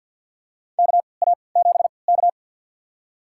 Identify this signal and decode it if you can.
Morse code, heard as a keyed tone, about 52 wpm, spaced as for 29 wpm, 700 Hz. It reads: KA7K
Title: USA Call Signs - 3x Character Spacing - 50wpm